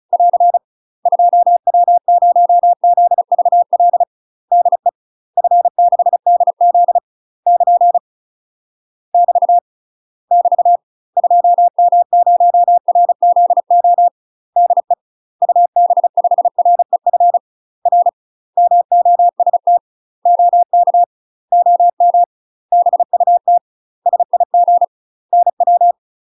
Morse code, heard as a keyed tone, about 34 wpm, 700 Hz. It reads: <AR> 2W0ZVL DE F6DZ <KN> <BT> <BT> 2M0RZO DE UB5REF R MOST OK OM BUT SIG NW
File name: UK MASTER QSOs TRAINING / 35WPM / UK MASTER QSOs 0024